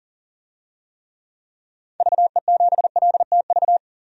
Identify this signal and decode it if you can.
Morse code, heard as a keyed tone, about 40 wpm, 700 Hz. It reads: VE7LTV
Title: DXCC-100 Call Signs - 6 Characters - 40wpm